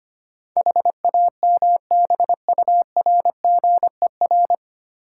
Morse code, heard as a keyed tone, about 25 wpm, 690 Hz. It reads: HAMBURGER
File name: practice / 25wpm / Easy Words with Hints - 25wpm